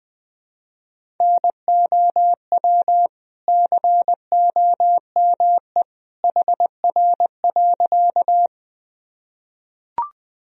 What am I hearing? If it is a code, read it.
Morse code, heard as a keyed tone, about 21 wpm, 700 Hz. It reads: NOW COME HR.
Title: Sentences from Top 200 Words and Abbreviations 20wpm